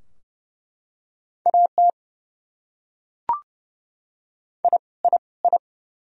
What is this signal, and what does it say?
Morse code, heard as a keyed tone, about 30 wpm, 700 Hz. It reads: AT I I I
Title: Top 100 Words Repeated 30wpm